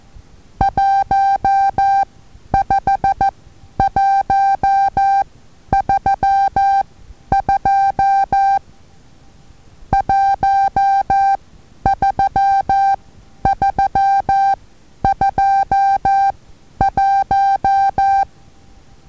Morse code, heard as a keyed tone, about 14 wpm, spaced as for 10 wpm, 790 Hz. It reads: 15132 13321
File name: Taviro HA6VA A